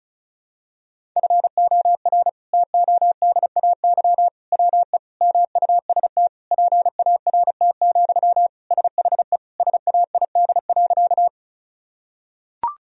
Morse code, heard as a keyed tone, about 36 wpm, 700 Hz. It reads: FOR TODAY WE MUST PART, SHE SAID.
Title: Sentences from Top 1000 Words - Review Part 1 - 35wpm